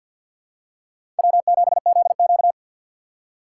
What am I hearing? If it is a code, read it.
Morse code, heard as a keyed tone, about 53 wpm, 700 Hz. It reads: W7ZQ
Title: USA Call Signs - 4x Character Spacing - 50wpm